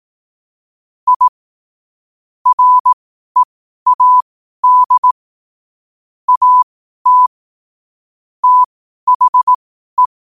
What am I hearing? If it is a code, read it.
Morse code, heard as a keyed tone, about 18 wpm, spaced as for 12 wpm, 1000 Hz. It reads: I READ AT THE
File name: qotdfiles / 12wpm